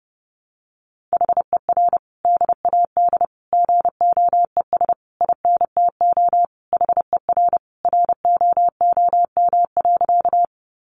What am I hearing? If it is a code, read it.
Morse code, heard as a keyed tone, about 30 wpm, 700 Hz. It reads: HER DAD GOES INTO HER ROOM.
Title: Additional Easy Phrases 30wpm